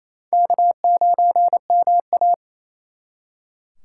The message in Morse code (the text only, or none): K9MA